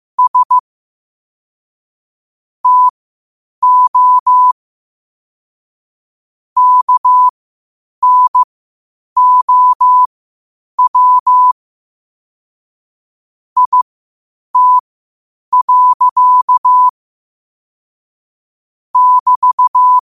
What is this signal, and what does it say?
Morse code, heard as a keyed tone, about 15 wpm, spaced as for 8 wpm, 1000 Hz. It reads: S TO KNOW IT. <BT>
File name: qotdfiles / 8wpm